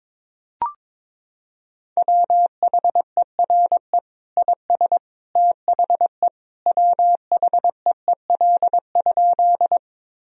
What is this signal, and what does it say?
Morse code, heard as a keyed tone, about 22 wpm, 700 Hz. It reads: WHERE IS THE WHEEL?
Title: Sentences from Top 1000 Words - Review Part 3 - 22wpm